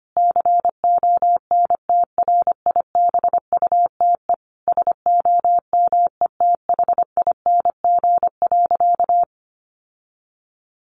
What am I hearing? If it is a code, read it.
Morse code, heard as a keyed tone, about 25 wpm, 700 Hz. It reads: CONTRIBUTE SOMETHING.
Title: Sentences from Top 1400 Words 25wpm